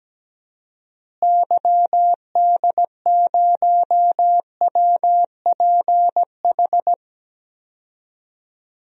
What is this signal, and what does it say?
Morse code, heard as a keyed tone, about 17 wpm, 700 Hz. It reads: YD0WPH
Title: DXCC-100 Call Signs - 6 Characters - Repeated 3x - 17wpm